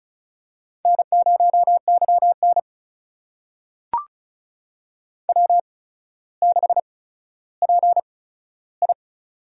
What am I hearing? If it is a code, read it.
Morse code, heard as a keyed tone, about 36 wpm, spaced as for 15 wpm, 700 Hz. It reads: N0YN W6PI
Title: USA Call Signs - 4 Characters - 8x Character Spacing - 35wpm